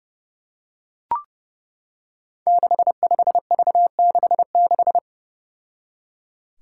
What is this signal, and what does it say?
Morse code, heard as a keyed tone, about 30 wpm, 700 Hz. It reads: 65V66
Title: Five Challenging Letter-Number Combinations 30wpm